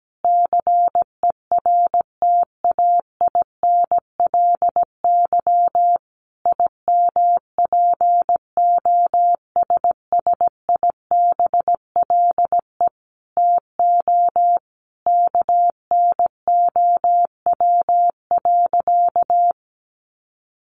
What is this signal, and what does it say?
Morse code, heard as a keyed tone, about 17 wpm, 700 Hz. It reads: CERTAINLY IMPOSSIBLE TO KNOW.
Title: Sentences from Top 1100 Words 17wpm